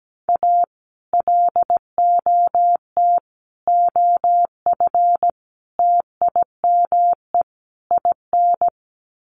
Morse code, heard as a keyed tone, about 17 wpm, 700 Hz. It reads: A LOT OF TIME IN
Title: Sets of 5 Words Encore2 17wpm